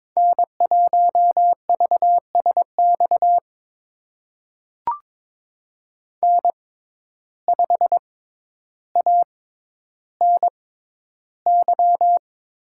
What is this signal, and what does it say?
Morse code, heard as a keyed tone, about 22 wpm, spaced as for 11 wpm, 700 Hz. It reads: N1VSX N5ANY
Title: USA Call Signs - 5 Characters - 6x Character Spacing - 22wpm